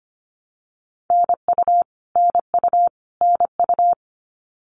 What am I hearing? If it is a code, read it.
Morse code, heard as a keyed tone, about 25 wpm, 700 Hz. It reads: NU NU NU
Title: Canadian Provinces and Territories - Repeated 3x - 25wpm